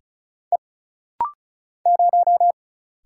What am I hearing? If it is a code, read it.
Morse code, heard as a keyed tone, about 33 wpm, 700 Hz. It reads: E 0
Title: Single Letter-Number - Rapid-Fire 35wpm